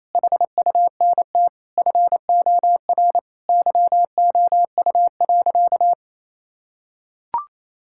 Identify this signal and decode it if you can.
Morse code, heard as a keyed tone, about 28 wpm, 700 Hz. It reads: HUNT FOR YOU.
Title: Sentences from Top 1600 Words 28wpm